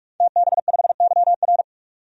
Morse code, heard as a keyed tone, about 41 wpm, 690 Hz. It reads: TB5YR
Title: DXCC-100 Call Signs - 5 Characters - 45wpm